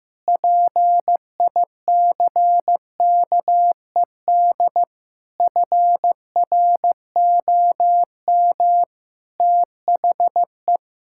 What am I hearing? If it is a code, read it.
Morse code, heard as a keyed tone, about 15 wpm, 700 Hz. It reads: PICKED FROM THE
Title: Pangram Challenge v2 - 15wpm